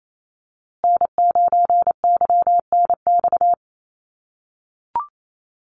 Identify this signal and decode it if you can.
Morse code, heard as a keyed tone, about 28 wpm, 700 Hz. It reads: N9YNX
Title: USA Call Signs Repeated 2x 28wpm